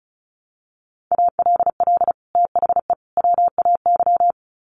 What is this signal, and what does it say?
Morse code, heard as a keyed tone, about 34 wpm, 700 Hz. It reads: ALL THE WAY
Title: Sets of 3 Words 35wpm